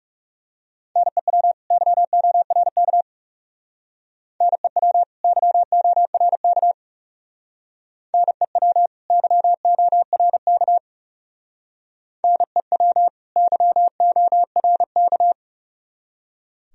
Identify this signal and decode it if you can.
Morse code, heard as a keyed tone, about 37 wpm, 700 Hz. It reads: NEW YORK NEW YORK NEW YORK NEW YORK
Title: Speed-Racing - Sets of 2 Words 30wpm